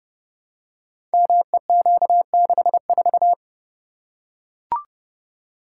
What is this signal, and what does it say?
Morse code, heard as a keyed tone, about 30 wpm, 700 Hz. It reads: MEQ64
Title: Code Groups - 5 Letter:Numbers - 30wpm